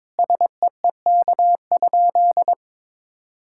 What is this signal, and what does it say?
Morse code, heard as a keyed tone, about 21 wpm, 700 Hz. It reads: SEEK?
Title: Sentences from Top 1600 Words 22wpm